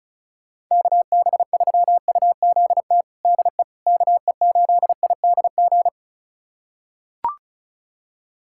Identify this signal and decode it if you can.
Morse code, heard as a keyed tone, about 36 wpm, 700 Hz. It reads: KB3UZT DE KE8IDG
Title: USA Callsign DE Callsign 35wpm